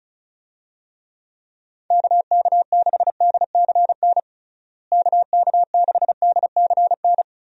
Morse code, heard as a keyed tone, about 34 wpm, 700 Hz. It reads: KK6DCN KK6DCN
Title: USA Call Signs - 6 Characters - Repeated 3x - 35wpm